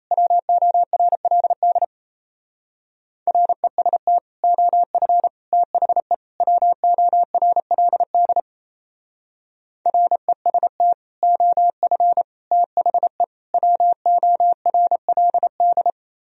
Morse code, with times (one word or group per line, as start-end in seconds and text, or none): 0.1-1.8 WORLD
3.3-4.2 REST
4.4-5.3 OF
5.5-6.1 THE
6.4-8.4 WORLD
9.9-10.9 REST
11.2-12.2 OF
12.5-13.2 THE
13.5-15.9 WORLD